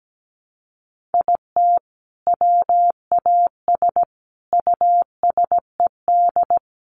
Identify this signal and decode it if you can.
Morse code, heard as a keyed tone, about 17 wpm, 700 Hz. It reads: IT WAS USED
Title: Sentences from Top 1400 Words 17wpm